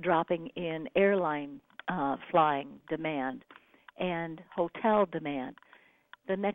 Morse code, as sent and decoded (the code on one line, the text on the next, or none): none